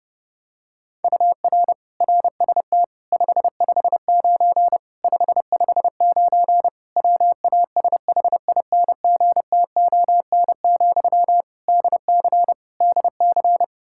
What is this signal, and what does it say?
Morse code, heard as a keyed tone, about 30 wpm, 700 Hz. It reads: UR RST 559 559 WASHINGTON, DC DC
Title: Practice QSO - Exchange 1 - Beginning - 30wpm